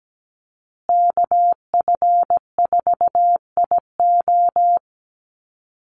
Morse code, heard as a keyed tone, about 17 wpm, 700 Hz. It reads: KF4IO
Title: USA Call Signs Repeated 2x 17wpm